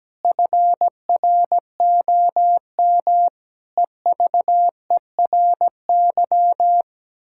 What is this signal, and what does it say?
Morse code, heard as a keyed tone, about 17 wpm, 700 Hz. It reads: FROM EVERY
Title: Pangram Challenge v2 - 17wpm